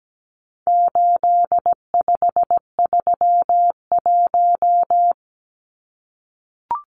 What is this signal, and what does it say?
Morse code, heard as a keyed tone, about 17 wpm, 700 Hz. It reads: 8531
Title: Numbers - 4-Digit 17wpm